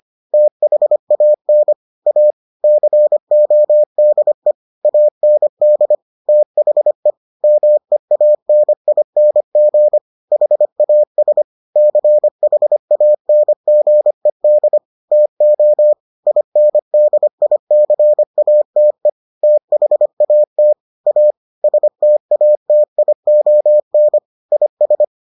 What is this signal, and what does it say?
Morse code, heard as a keyed tone, about 25 wpm, 600 Hz. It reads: THAN A CODE AND THE MEANING HAS CHANGED TO INDICATE THAT A STATION IS